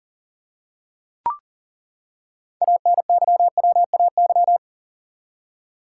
Morse code, heard as a keyed tone, about 37 wpm, 700 Hz. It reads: ANYWAY
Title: Six Letter Words - 40wpm